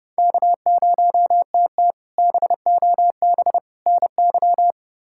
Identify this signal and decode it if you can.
Morse code, heard as a keyed tone, about 30 wpm, 700 Hz. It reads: K0TT BOB NY
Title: K1USN SST QSO Practice - 30wpm